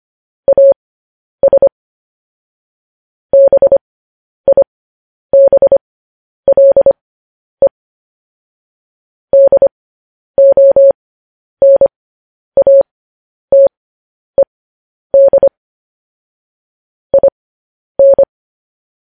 Morse code, heard as a keyed tone, about 25 wpm, spaced as for 10 wpm, 550 Hz. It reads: AS BIBLE DONATED IN
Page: news-headlines-in-morse-code-at-10-wpm